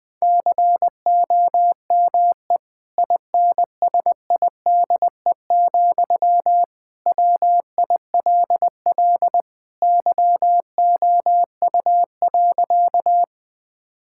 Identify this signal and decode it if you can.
Morse code, heard as a keyed tone, about 21 wpm, 700 Hz. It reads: COME INSIDE, WILL YOU.
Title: Sentences from Top 1000 Words - Review Part 6 - 20wpm